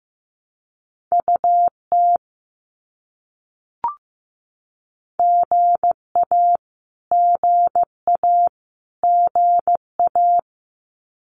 Morse code, heard as a keyed tone, about 15 wpm, 700 Hz. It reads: UT GA GA GA
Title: US States Repeated 3x 15wpm